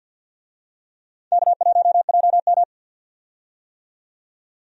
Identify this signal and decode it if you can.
Morse code, heard as a keyed tone, about 51 wpm, 700 Hz. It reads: K1JM
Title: USA Call Signs - 4 Characters - 50wpm